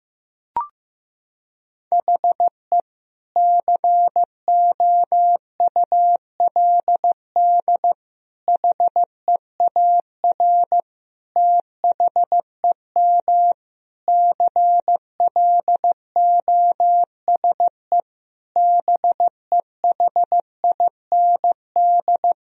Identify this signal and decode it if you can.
Morse code, heard as a keyed tone, about 15 wpm, 700 Hz. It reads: HE COULD HEAR THEM CLOSE BEHIND
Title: Sentences from Top 1000 Words - Review Part 1 - 15wpm